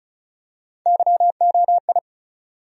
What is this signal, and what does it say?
Morse code, heard as a keyed tone, about 36 wpm, 700 Hz. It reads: YOI
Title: USA Call Sign Postfixes - 3 Letters 35wpm